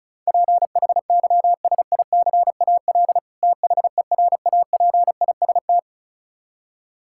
Morse code, heard as a keyed tone, about 36 wpm, 700 Hz. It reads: PHYSICAL THERAPIST
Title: Practice QSO - Exchange 3 - More Info - 35wpm